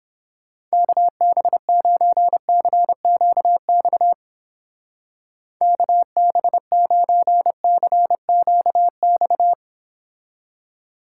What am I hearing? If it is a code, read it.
Morse code, heard as a keyed tone, about 29 wpm, 700 Hz. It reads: KB9CQX KB9CQX